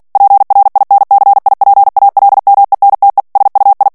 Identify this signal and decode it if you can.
Morse code, heard as a keyed tone, about 48 wpm, 800 Hz. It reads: PRINCIPALMENTE SUI
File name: morse mp3 / ITALIEN texte clair 1 vitesse 45 mots minute